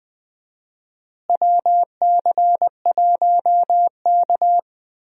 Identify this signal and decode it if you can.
Morse code, heard as a keyed tone, about 20 wpm, 700 Hz. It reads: WC1K
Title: USA Call Signs - 4 Characters - Repeated 2x - 20wpm